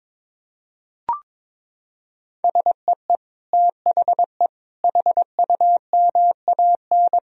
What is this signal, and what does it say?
Morse code, heard as a keyed tone, about 22 wpm, 700 Hz. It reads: SEE THE HUMAN
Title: Sentences from Top 1000 Words - Review Part 4 - 22wpm